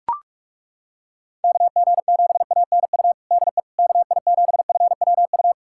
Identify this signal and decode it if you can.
Morse code, heard as a keyed tone, about 42 wpm, 700 Hz. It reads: KG7ANU DE KI7FWU
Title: USA Callsign DE Callsign - 6 Characters - 45wpm